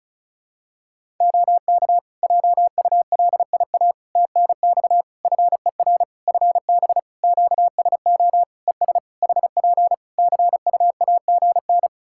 Morse code, heard as a keyed tone, about 34 wpm, 700 Hz. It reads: OK JULIA TNX FER FB QSO ES HP CUAGN
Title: Practice QSO - Exchange 4 - Ending - 35wpm